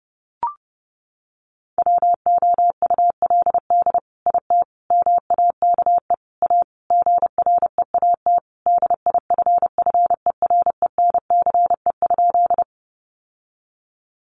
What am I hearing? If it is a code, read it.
Morse code, heard as a keyed tone, about 30 wpm, 700 Hz. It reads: WOULD IT MAKE A GREAT DIFFERENCE?
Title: Sentences from Top 800 Words 30wpm